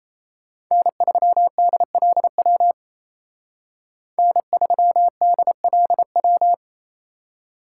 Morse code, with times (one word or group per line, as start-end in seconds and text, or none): 0.7-2.7 N3DLW
4.2-6.5 N3DLW